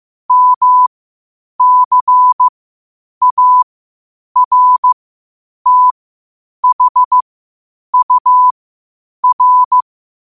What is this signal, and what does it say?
Morse code, heard as a keyed tone, about 15 wpm, spaced as for 8 wpm, 1000 Hz. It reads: MCARTHUR
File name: qotdfiles / 8wpm